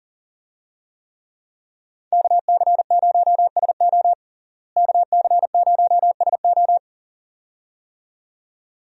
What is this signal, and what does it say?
Morse code, heard as a keyed tone, about 40 wpm, 700 Hz. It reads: KC0SO KC0SO
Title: USA Call Signs - 5 Characters - Repeated 2x - 40wpm